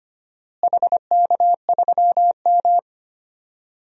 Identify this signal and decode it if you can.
Morse code, heard as a keyed tone, about 25 wpm, 700 Hz. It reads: HK3M
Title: DXCC-100 Call Signs - 4 Characters - Repeated 3x - 25wpm